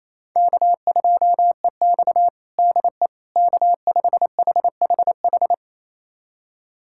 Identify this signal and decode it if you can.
Morse code, heard as a keyed tone, about 28 wpm, 700 Hz. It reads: K2EX DE K5HHH
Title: USA Callsign DE Callsign 28wpm